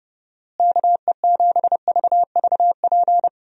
Morse code, heard as a keyed tone, about 31 wpm, 700 Hz. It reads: KE7VVP
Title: USA Call Signs - 6 Characters - Repeated 3x - 30wpm